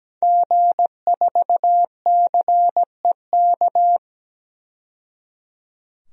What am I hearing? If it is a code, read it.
Morse code, heard as a keyed tone, about 17 wpm, 700 Hz. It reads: G4CEK
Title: DXCC-100 Call Signs - 5 Characters - 17wpm